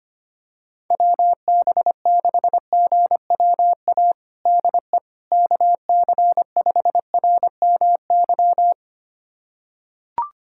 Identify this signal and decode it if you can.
Morse code, heard as a keyed tone, about 25 wpm, 700 Hz. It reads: WB6GWA DE KC5RMY
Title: USA Callsign DE Callsign 25wpm